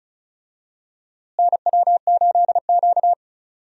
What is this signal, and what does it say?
Morse code, heard as a keyed tone, about 34 wpm, 700 Hz. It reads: NW8Q
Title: USA Call Signs - 4 Characters - Repeated 3x - 35wpm